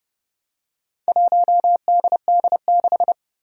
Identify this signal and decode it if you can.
Morse code, heard as a keyed tone, about 32 wpm, 700 Hz. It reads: 1DD6
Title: Four Challenging Letter-Number Combinations 30wpm